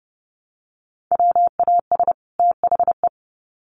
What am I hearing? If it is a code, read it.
Morse code, heard as a keyed tone, about 30 wpm, 700 Hz. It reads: WAS THE
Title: Sets of 2 Words 30wpm